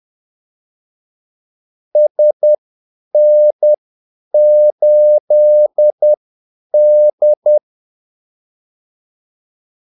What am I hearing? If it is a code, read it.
Morse code, heard as a keyed tone, about 10 wpm, spaced as for 8 wpm, 600 Hz. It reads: SN8D
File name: Callsigns-10-8wpm